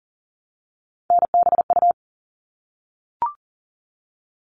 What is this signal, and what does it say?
Morse code, heard as a keyed tone, about 40 wpm, 700 Hz. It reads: NBU
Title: All Cut Numbers - 3-Digit 40wpm